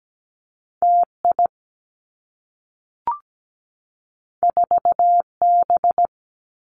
Morse code, heard as a keyed tone, about 17 wpm, 700 Hz. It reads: TI 4B
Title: Two Challenging Letter-Number Combinations 17wpm